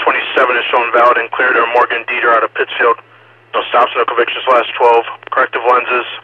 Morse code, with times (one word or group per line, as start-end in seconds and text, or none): none